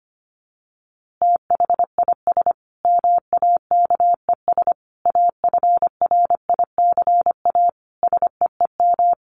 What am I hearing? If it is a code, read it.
Morse code, heard as a keyed tone, about 25 wpm, 700 Hz. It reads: THIS MAKES AFRICA SEEM
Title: A Sound Of Thunder 25wpm